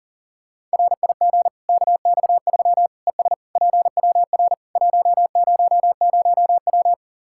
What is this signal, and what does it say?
Morse code, heard as a keyed tone, about 40 wpm, 700 Hz. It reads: RIG KX3 ES PWR 100W
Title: Practice QSO - Exchange 2 - Info - 40wpm